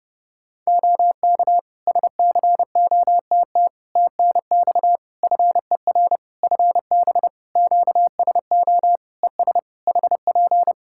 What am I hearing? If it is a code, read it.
Morse code, heard as a keyed tone, about 30 wpm, 700 Hz. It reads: OK SCOTT TNX FER FB QSO ES HP
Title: Practice QSO - Exchange 4 - Ending - 30wpm